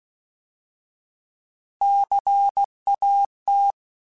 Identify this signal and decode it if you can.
Morse code, heard as a keyed tone, about 16 wpm, 800 Hz. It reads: CAT
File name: Alphabetical / Alphabetical mp3s / Cs @15